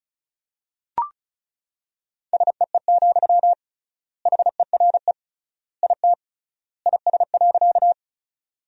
Morse code, heard as a keyed tone, about 36 wpm, 700 Hz. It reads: SEE, HERE IT IS.
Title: 3x Word-Spacing - Sentences from Top 200 Words - 35wpm